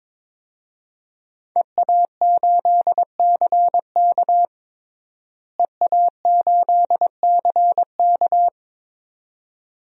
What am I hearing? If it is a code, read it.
Morse code, heard as a keyed tone, about 22 wpm, 700 Hz. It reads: EA8CK EA8CK
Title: DXCC-100 Call Signs - 5 Characters - Repeated 2x - 22wpm